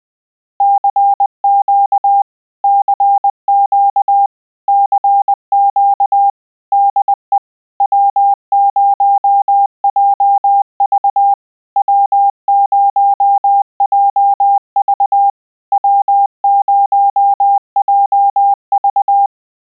Morse code, heard as a keyed tone, about 20 wpm, 800 Hz. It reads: CQ CQ CQ DE W0JV W0JV W0JV